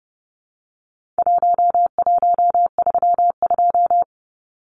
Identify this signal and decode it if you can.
Morse code, heard as a keyed tone, about 30 wpm, 700 Hz. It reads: 1132
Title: Numbers - Contest Style - 4-Digit 30wpm